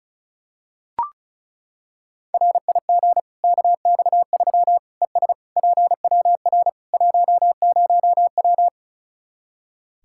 Morse code, heard as a keyed tone, about 36 wpm, 700 Hz. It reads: RIG KX3 ES PWR 10W
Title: Practice QSO - Exchange 2 - Info - 35wpm